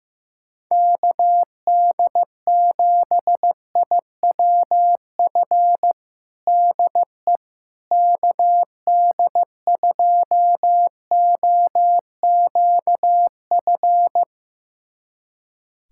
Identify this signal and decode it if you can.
Morse code, heard as a keyed tone, about 15 wpm, 700 Hz. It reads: KD7IWF DE KD2OQF
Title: USA Callsign DE Callsign - 6 Characters - 15wpm